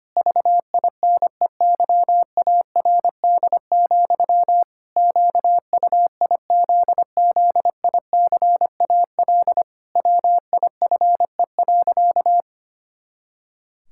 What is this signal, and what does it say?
Morse code, heard as a keyed tone, about 25 wpm, 700 Hz. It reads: VINEYARD, QUIZZICAL WIFE.
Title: Pangram Challenge v2 - 25wpm